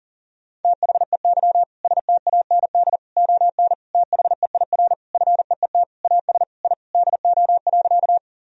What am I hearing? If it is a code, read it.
Morse code, heard as a keyed tone, about 40 wpm, 700 Hz. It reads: THEY STAND ON THEIR FEET AS I DO.
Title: Sentences from Top 1000 Words - Review Part 1 - 40wpm